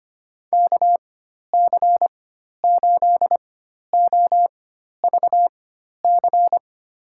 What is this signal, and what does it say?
Morse code, heard as a keyed tone, about 25 wpm, spaced as for 12 wpm, 700 Hz. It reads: KC8OVC
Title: USA Call Signs - 6 Characters - 4x Character Spacing - 25wpm